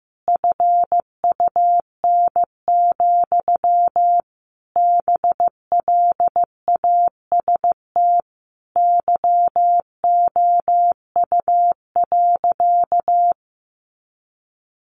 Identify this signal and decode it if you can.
Morse code, heard as a keyed tone, about 15 wpm, 700 Hz. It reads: FUN, BLAST YOU.
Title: A Sound Of Thunder 15wpm